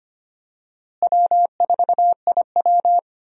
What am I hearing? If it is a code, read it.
Morse code, heard as a keyed tone, about 25 wpm, 700 Hz. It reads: W4IW
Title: USA Call Signs - 4 Characters - Repeated 2x - 25wpm